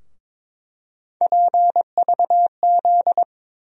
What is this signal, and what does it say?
Morse code, heard as a keyed tone, about 22 wpm, 700 Hz. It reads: PVZ
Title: USA Call Sign Postfixes - 3 Letters 22wpm